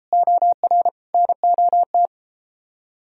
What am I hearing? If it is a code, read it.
Morse code, heard as a keyed tone, about 33 wpm, 700 Hz. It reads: OR NOT